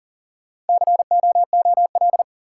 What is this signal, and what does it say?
Morse code, heard as a keyed tone, about 40 wpm, 700 Hz. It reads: COOL
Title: Sentences from Top 1200 Words - New Word Drill - 40wpm